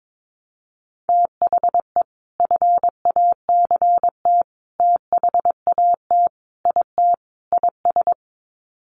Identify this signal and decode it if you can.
Morse code, heard as a keyed tone, about 22 wpm, 700 Hz. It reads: THE FACT THAT IT IS
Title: Sets of 5 Words Encore2 22wpm